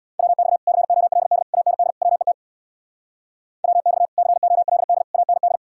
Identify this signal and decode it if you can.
Morse code, heard as a keyed tone, about 67 wpm, 700 Hz. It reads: WX COLD AND WET WX COLD AND